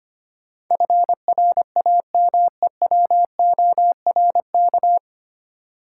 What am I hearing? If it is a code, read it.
Morse code, heard as a keyed tone, about 25 wpm, 700 Hz. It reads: FRAMEWORK
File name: Nine Letter Words - 25wpm